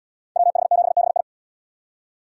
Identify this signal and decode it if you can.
Morse code, heard as a keyed tone, about 73 wpm, 700 Hz. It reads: WH7BI